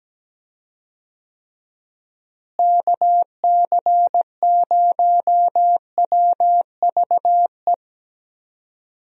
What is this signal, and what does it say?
Morse code, heard as a keyed tone, about 17 wpm, 700 Hz. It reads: KC0WVE
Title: USA Call Signs - 6 Characters - Repeated 2x - 17wpm